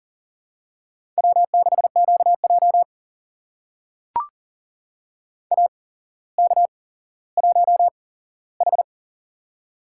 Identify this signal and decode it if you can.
Morse code, heard as a keyed tone, about 40 wpm, spaced as for 17 wpm, 700 Hz. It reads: W6QJ AK1H
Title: USA Call Signs - 8x Character Spacing - 40wpm